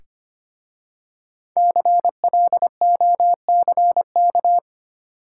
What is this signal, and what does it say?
Morse code, heard as a keyed tone, about 25 wpm, 700 Hz. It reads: CLOCK
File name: practice / 25wpm / Easy Words with Hints - 25wpm